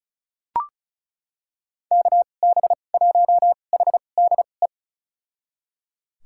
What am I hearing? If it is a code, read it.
Morse code, heard as a keyed tone, about 34 wpm, spaced as for 25 wpm, 700 Hz. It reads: KB1HDE
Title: USA Call Signs - 6 Characters - 2x Character Spacing - 35wpm